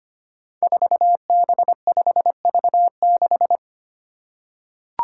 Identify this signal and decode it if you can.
Morse code, heard as a keyed tone, about 25 wpm, 700 Hz. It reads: 4B5V6
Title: Five Challenging Letter-Number Combinations 25wpm